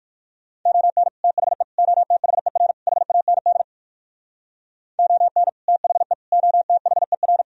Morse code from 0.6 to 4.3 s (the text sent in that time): ON THE OTHER HAND